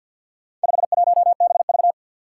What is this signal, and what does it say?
Morse code, heard as a keyed tone, about 46 wpm, 700 Hz. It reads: 51BV
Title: Four Challenging Letter-Number Combinations 50wpm